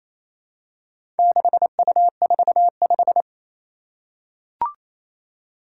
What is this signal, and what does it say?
Morse code, heard as a keyed tone, about 28 wpm, 700 Hz. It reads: 6U45
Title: Four Challenging Letter-Number Combinations 28wpm